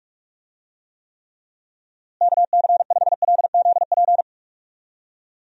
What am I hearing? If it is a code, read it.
Morse code, heard as a keyed tone, about 41 wpm, 700 Hz. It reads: KC5LZP
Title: USA Call Signs - 6 Characters - Repeated 2x - 45wpm